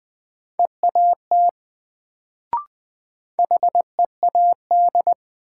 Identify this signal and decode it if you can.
Morse code, heard as a keyed tone, about 20 wpm, 700 Hz. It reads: EAT HEAD
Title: Sentences from Top 300 Words - New Word Drill - 20wpm